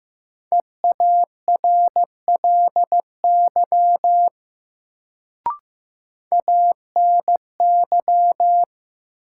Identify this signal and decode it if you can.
Morse code, heard as a keyed tone, about 15 wpm, 700 Hz. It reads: EARLY ANY
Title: Sets of 3 Words - New Word Drill - 15wpm